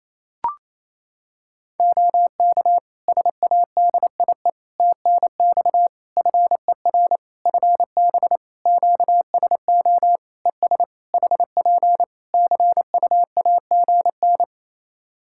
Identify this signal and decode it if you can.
Morse code, heard as a keyed tone, about 28 wpm, 700 Hz. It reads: OK SADIE TNX FER FB QSO ES HP CUAGN
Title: Practice QSO - Exchange 4 - Ending - 28wpm